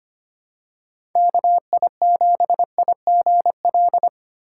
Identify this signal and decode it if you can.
Morse code, heard as a keyed tone, about 25 wpm, 700 Hz. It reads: KI7IGL
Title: USA Call Signs - 3x Character Spacing - 25wpm